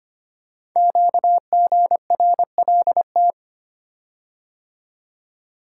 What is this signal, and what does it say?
Morse code, heard as a keyed tone, about 25 wpm, 700 Hz. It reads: QGRLT
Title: Code Groups - 5 Letters - 25wpm